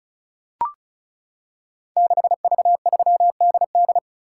Morse code, heard as a keyed tone, about 36 wpm, 700 Hz. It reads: 6V3DD
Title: Code Groups - 5 Letter:Numbers - 35wpm